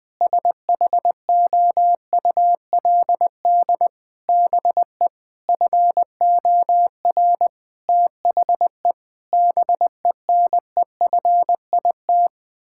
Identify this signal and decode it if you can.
Morse code, heard as a keyed tone, about 20 wpm, 700 Hz. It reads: SHOULD BE FOR THE BENEFIT
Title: Sentences from Top 1000 Words - Review Part 6 - 20wpm